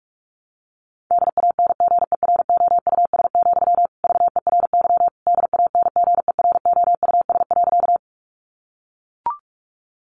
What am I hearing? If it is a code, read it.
Morse code, heard as a keyed tone, about 44 wpm, 700 Hz. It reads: DANGEROUS, VERY DANGEROUS.
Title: Sentences from Top 1200 Words 45wpm